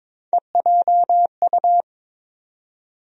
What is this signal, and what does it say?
Morse code, heard as a keyed tone, about 22 wpm, 700 Hz. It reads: EJU
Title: USA Call Sign Postfixes - 3 Letters 22wpm